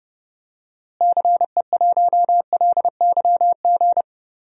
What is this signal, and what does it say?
Morse code, heard as a keyed tone, about 32 wpm, 700 Hz. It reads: CE1LYG
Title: DXCC-100 Call Signs - 6 Characters - Repeated 3x - 30wpm